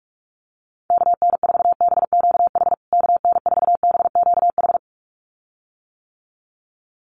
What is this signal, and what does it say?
Morse code, heard as a keyed tone, about 44 wpm, 700 Hz. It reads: KN4BQH KN4BQH
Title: USA Call Signs Repeated 2x 45wpm